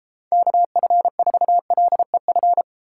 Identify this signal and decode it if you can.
Morse code, heard as a keyed tone, about 30 wpm, 700 Hz. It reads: KF4LEF